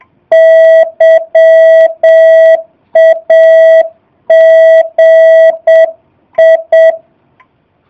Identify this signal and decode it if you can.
Morse code, heard as a keyed tone, about 7 wpm, 660 Hz. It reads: YAGI
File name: jh8cbh / 2019cw / SP05- w